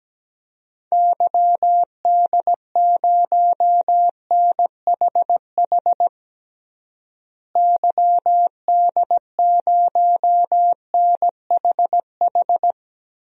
Morse code, heard as a keyed tone, about 17 wpm, 700 Hz. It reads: YD0NHH YD0NHH
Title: DXCC-100 Call Signs - 6 Characters - Repeated 2x - 17wpm